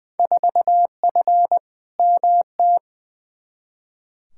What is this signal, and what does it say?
Morse code, heard as a keyed tone, about 20 wpm, 700 Hz. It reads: 4F MT
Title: Field Day Exchanges Encore - 20wpm